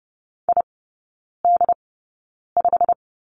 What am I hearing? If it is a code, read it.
Morse code, heard as a keyed tone, about 30 wpm, spaced as for 9 wpm, 700 Hz. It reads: ID5
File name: course / 30:10wpm / 087 Lesson 15e - Number 5 - Three Characters 30wpm